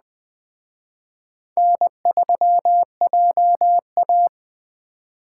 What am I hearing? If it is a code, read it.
Morse code, heard as a keyed tone, about 19 wpm, 700 Hz. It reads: N3JA